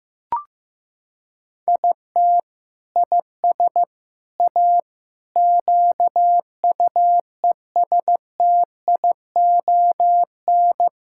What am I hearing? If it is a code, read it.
Morse code, heard as a keyed tone, about 14 wpm, 700 Hz. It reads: IT IS A QUESTION
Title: Sentences from Top 1000 Words - Review Part 6 - 15wpm